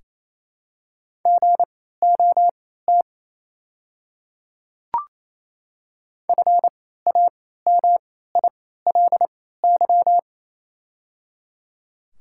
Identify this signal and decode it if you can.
Morse code, heard as a keyed tone, about 28 wpm, spaced as for 16 wpm, 700 Hz. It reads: GOT FAMILY
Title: 3x Character Spacing - Top 500 Words - 28wpm